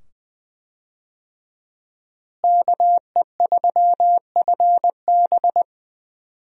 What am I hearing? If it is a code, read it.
Morse code, heard as a keyed tone, about 20 wpm, 700 Hz. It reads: KE3FB
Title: USA Call Signs - 5 Characters - Repeated 2x - 20wpm